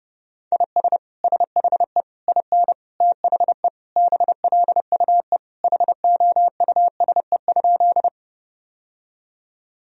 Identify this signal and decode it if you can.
Morse code, heard as a keyed tone, about 32 wpm, 700 Hz. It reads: IS SHE IN THE BLUE HOUSE?
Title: Sentences from Top 1000 Words - Review Part 3 - 30wpm